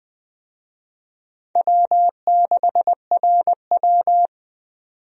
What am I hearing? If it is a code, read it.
Morse code, heard as a keyed tone, about 20 wpm, 700 Hz. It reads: W6RW
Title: USA Call Signs - 4 Characters - 20wpm